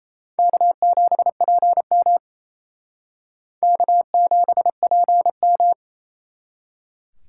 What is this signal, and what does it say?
Morse code, heard as a keyed tone, about 35 wpm, 700 Hz. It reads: K7PM K7PM